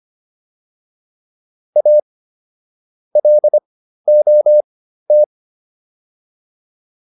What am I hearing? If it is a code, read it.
Morse code, heard as a keyed tone, about 25 wpm, spaced as for 13 wpm, 600 Hz. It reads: A LOT